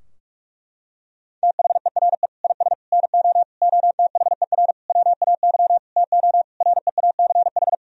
Morse code, heard as a keyed tone, about 46 wpm, 700 Hz. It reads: THERE IS NO OTHER WAY TO REACH
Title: Sentences from Top 1000 Words - Review Part 2 - 45wpm